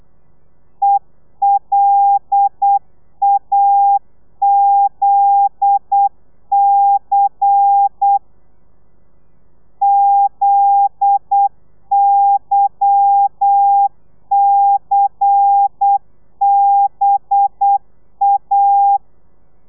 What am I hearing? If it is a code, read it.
Morse code, heard as a keyed tone, about 8 wpm, 800 Hz. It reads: ELAZC ZYCBA
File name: telegrafia / i2tfj / Lez10 35